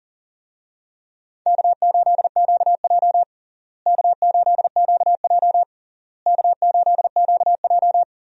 Morse code, heard as a keyed tone, about 40 wpm, 700 Hz. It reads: K8QJ K8QJ K8QJ
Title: USA Call Signs - 4 Characters - Repeated 3x - 40wpm